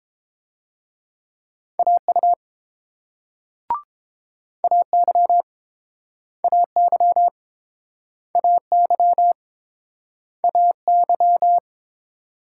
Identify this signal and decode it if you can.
Morse code, heard as a keyed tone, about 31 wpm, 700 Hz. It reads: AU AY AY AY AY